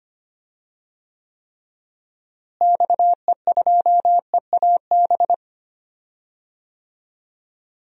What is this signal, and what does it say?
Morse code, heard as a keyed tone, about 25 wpm, 700 Hz. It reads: XE2EAB
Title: DXCC-100 Call Signs - 6 Characters - Repeated 3x - 25wpm